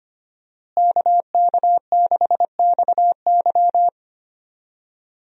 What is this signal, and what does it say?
Morse code, heard as a keyed tone, about 25 wpm, 700 Hz. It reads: KK6XY
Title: USA Call Signs - 5 Characters - Repeated 2x - 25wpm